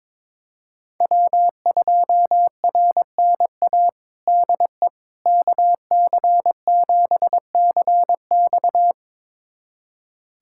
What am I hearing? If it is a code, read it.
Morse code, heard as a keyed tone, about 22 wpm, 700 Hz. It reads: W2RNA DE KC7CX
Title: USA Callsign DE Callsign - 5 Characters - 22wpm